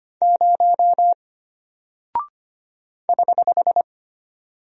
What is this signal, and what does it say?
Morse code, heard as a keyed tone, about 25 wpm, 700 Hz. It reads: 0 <HH>
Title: Single Character (Letter:Number:Punctuation:Prosign) Alt 25wpm